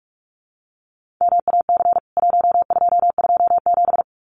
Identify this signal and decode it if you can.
Morse code, heard as a keyed tone, about 44 wpm, 700 Hz. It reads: MAC 1227
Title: CWT Exchange 45wpm